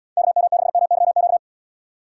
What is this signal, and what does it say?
Morse code, heard as a keyed tone, about 78 wpm, 700 Hz. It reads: KM6MYQ